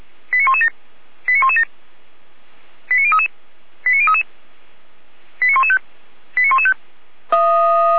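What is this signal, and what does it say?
Five-tone selective call (ZVEI), heard as quick runs of five tones, tones of about 80 ms. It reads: 89117 89117 89033 89033 89115 89115
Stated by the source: ZVEI five-tone selective call received on an SDR.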